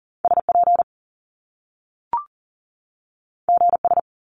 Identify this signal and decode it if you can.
Morse code, heard as a keyed tone, about 40 wpm, 700 Hz. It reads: SP GS
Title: Top Two Letter Combinations in Words 40wpm